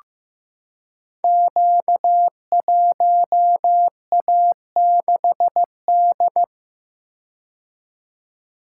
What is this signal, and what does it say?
Morse code, heard as a keyed tone, about 15 wpm, 700 Hz. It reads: Q1A6D
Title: Code Groups - 5 Letter:Numbers - 15wpm